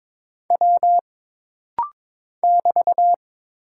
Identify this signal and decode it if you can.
Morse code, heard as a keyed tone, about 22 wpm, 700 Hz. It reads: W <BT>
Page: This rapid-fire timing is geared to push your ICR (Instant Character Recognition) skills.